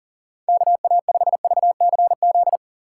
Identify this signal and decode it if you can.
Morse code, heard as a keyed tone, about 40 wpm, 700 Hz. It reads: KA5VCZ
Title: USA Call Signs - 6x Character Spacing - 40wpm